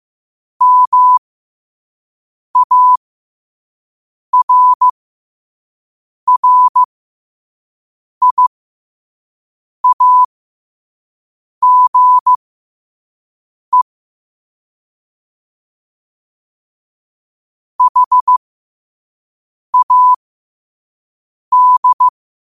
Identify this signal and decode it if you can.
Morse code, heard as a keyed tone, about 15 wpm, spaced as for 5 wpm, 1000 Hz. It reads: MARRIAGE HAD
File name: 5wpm